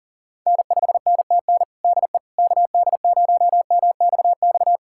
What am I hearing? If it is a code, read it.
Morse code, heard as a keyed tone, about 40 wpm, 700 Hz. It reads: N5NTN DE KD0MXX
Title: USA Callsign DE Callsign 40wpm